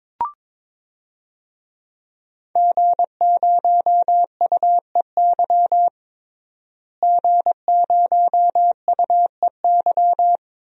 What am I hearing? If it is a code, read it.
Morse code, heard as a keyed tone, about 23 wpm, 700 Hz. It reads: G0UEY G0UEY
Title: DXCC-100 Call Signs - 5 Characters - Repeated 2x - 22wpm